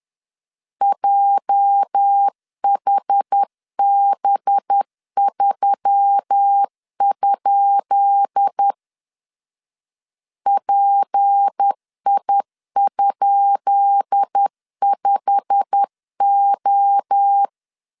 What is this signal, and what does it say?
Morse code, heard as a keyed tone, about 10 wpm, 800 Hz. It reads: JHB3? PI?5O
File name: LEZIONI / 10 SIGNS / 10B INT 10wpm 2844